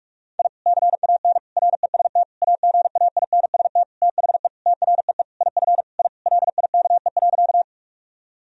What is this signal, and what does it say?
Morse code, heard as a keyed tone, about 46 wpm, 700 Hz. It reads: I CAN REST AGAINST THE TREE IF I LIKE.
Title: Sentences from Top 1000 Words - Review Part 1 - 45wpm